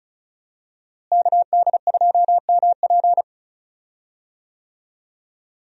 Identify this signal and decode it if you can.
Morse code, heard as a keyed tone, about 36 wpm, 700 Hz. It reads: KD2MP
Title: USA Call Signs - 5 Characters - Repeated 2x - 35wpm